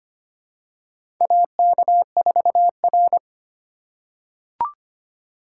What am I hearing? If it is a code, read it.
Morse code, heard as a keyed tone, about 25 wpm, 700 Hz. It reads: AK4R
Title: USA Call Signs - 8x Character Spacing - 25wpm